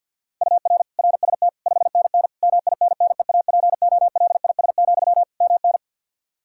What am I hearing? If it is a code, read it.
Morse code, heard as a keyed tone, about 47 wpm, 700 Hz. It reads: UR RST 5NN MINNEAPOLIS, MN
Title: Practice QSO - Exchange 1 - Beginning - 50wpm